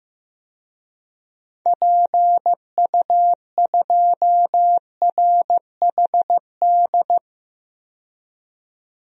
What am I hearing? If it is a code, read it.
Morse code, heard as a keyed tone, about 15 wpm, 700 Hz. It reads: PU2RHD
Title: DXCC-100 Call Signs - 6 Characters - 15wpm